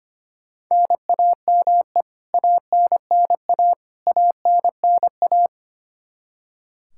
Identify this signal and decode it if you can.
Morse code, heard as a keyed tone, about 25 wpm, 700 Hz. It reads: NAME ANNA ANNA
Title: Practice QSO - Exchange 1 - Beginning - 25wpm